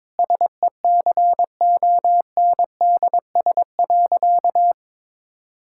Morse code, heard as a keyed tone, about 22 wpm, 690 Hz. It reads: SECONDS.
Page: morse-code-ninja-practice-22wpm